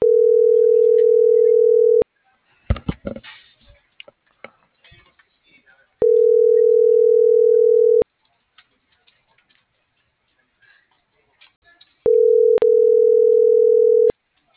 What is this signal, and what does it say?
Ringback tone (North America), 440 + 480 Hz.